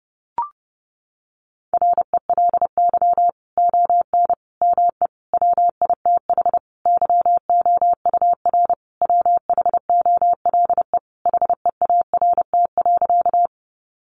Morse code, heard as a keyed tone, about 28 wpm, 700 Hz. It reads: RELY ON ME WITH YOUR WHOLE HEART.
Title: Sentences from Top 1200 Words 30wpm